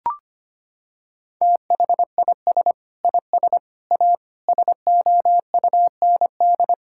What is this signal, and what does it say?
Morse code, heard as a keyed tone, about 25 wpm, 700 Hz. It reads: THIS IS A SOUND
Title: Sentences from Top 1000 Words - Review Part 6 - 25wpm